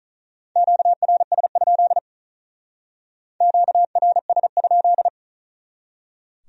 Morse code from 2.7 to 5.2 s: QRS?

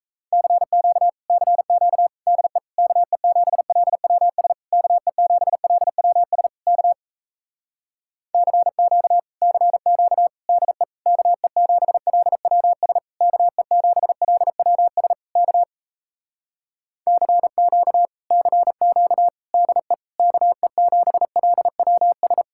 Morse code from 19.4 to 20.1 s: DE

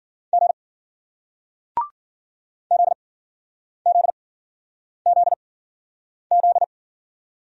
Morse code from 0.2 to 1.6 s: G

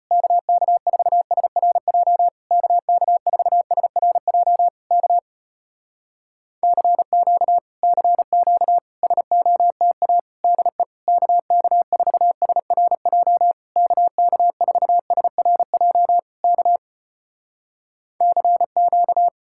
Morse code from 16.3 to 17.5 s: K

